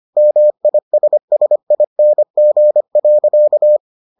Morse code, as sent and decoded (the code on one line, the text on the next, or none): -- .. ... ... .. -. --. .-.-.-
MISSING.